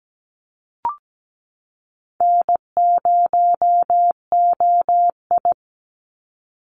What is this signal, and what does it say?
Morse code, heard as a keyed tone, about 17 wpm, 700 Hz. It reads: N0OI
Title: NA SOTA Activator Callsigns 17wpm